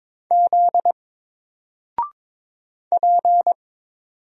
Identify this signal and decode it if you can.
Morse code, heard as a keyed tone, about 22 wpm, 700 Hz. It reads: Z P
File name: Single-Letter-Number-22wpm